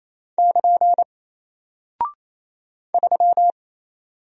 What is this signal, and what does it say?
Morse code, heard as a keyed tone, about 27 wpm, 700 Hz. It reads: <KN> 3
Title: Single Character (Letter:Number:Punctuation:Prosign) 28wpm